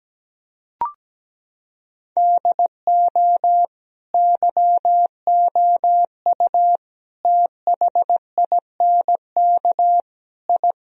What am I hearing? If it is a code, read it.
Morse code, heard as a keyed tone, about 17 wpm, 700 Hz. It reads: DO YOU THINK I
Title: Sentences from Top 1000 Words - Review Part 2 - 17wpm